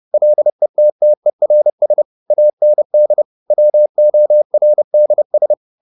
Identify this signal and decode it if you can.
Morse code, heard as a keyed tone, about 30 wpm, 600 Hz. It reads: LETTERS AND WORDS